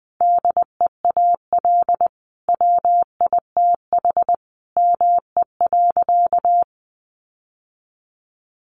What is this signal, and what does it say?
Morse code, heard as a keyed tone, about 20 wpm, 700 Hz. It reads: DEAL WITH ME.
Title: Sentences from Top 800 Words 20wpm